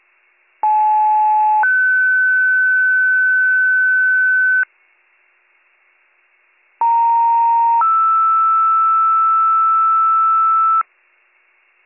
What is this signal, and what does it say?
Two-tone page, heard as a one-second tone, then a three-second tone, about 860 then 1500 Hz, about 930 then 1320 Hz. It is heard from 0.6 to 10.8 s.